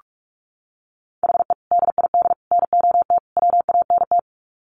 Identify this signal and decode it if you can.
Morse code, heard as a keyed tone, about 44 wpm, 700 Hz. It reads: HE DID NOT WANT
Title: Sets of 4 Words 45wpm